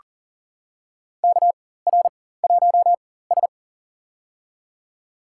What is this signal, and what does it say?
Morse code, heard as a keyed tone, about 40 wpm, spaced as for 19 wpm, 700 Hz. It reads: KR1S
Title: USA Call Signs - 4 Characters - 4x Character Spacing - 40wpm